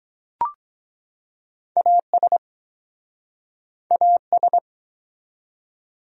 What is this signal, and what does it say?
Morse code, heard as a keyed tone, about 24 wpm, 700 Hz. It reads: AS AS